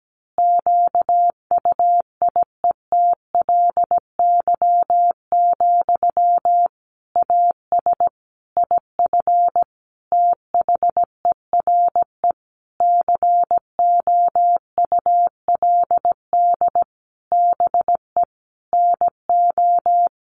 Morse code, heard as a keyed tone, about 17 wpm, 700 Hz. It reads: QUIETLY, AS IF THERE COULD BE NO
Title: A Sound Of Thunder 17wpm